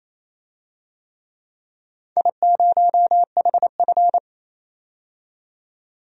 Morse code, heard as a keyed tone, about 27 wpm, 700 Hz. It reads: I0HF
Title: DXCC-100 Call Signs - 4 Characters - 28wpm